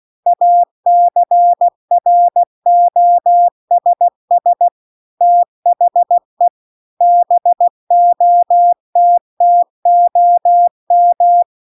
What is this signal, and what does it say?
Morse code, heard as a keyed tone, about 16 wpm, 700 Hz. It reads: ACROSS THE BOTTOM
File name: CW-The-Big-Twoheaded-River-2-16WPM